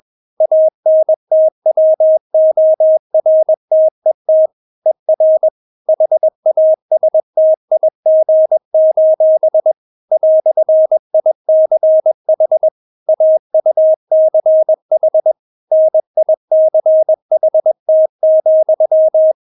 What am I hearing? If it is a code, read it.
Morse code, heard as a keyed tone, about 21 wpm, 630 Hz. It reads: ANTWORTET ER HASTIG: "ICH AUCH NICHT,
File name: Witz2 21wpm